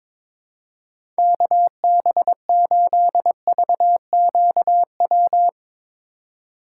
Morse code, heard as a keyed tone, about 21 wpm, 700 Hz. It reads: KB8VQW
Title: USA Call Signs - 6 Characters - 3x Character Spacing - 22wpm